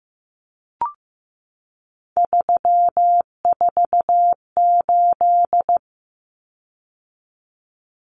Morse code, heard as a keyed tone, about 15 wpm, 700 Hz. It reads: 348
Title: Common Cut Numbers - 3-Digit 15wpm